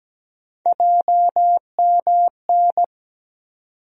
Morse code, heard as a keyed tone, about 17 wpm, 700 Hz. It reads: JMN
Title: USA Call Sign Postfixes - 3 Letters 17wpm